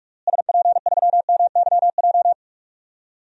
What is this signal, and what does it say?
Morse code, heard as a keyed tone, about 46 wpm, 700 Hz. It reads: SP3MYJ
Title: DXCC-100 Call Signs - 6 Characters - Repeated 3x - 45wpm